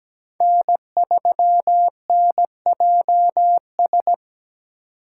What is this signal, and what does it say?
Morse code, heard as a keyed tone, about 20 wpm, 700 Hz. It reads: N3NJS
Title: Speed Racing - USA 1x3 Call Signs 17wpm